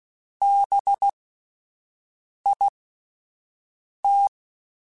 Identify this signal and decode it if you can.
Morse code, heard as a keyed tone, about 16 wpm, spaced as for 5 wpm, 800 Hz. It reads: BIT
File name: Spaced Alphas / Spaced Alphas mp3s / Bs sp @10